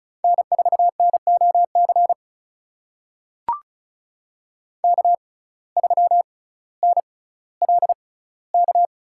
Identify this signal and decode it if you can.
Morse code, heard as a keyed tone, about 34 wpm, spaced as for 18 wpm, 700 Hz. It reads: N4NOC K3NLK
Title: USA Call Signs - 5 Characters - 6x Character Spacing - 35wpm